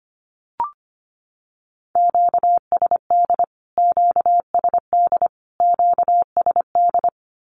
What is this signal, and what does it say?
Morse code, heard as a keyed tone, about 25 wpm, 700 Hz. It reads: QSD QSD QSD
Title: Q-Codes Repeated 3x 25wpm